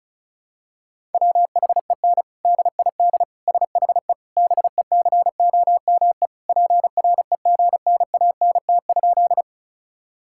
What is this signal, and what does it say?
Morse code, heard as a keyed tone, about 36 wpm, 700 Hz. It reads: WHEN DID SHE BECOME PREGNANT?
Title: Sentences from Top 1000 Words - Review Part 6 - 35wpm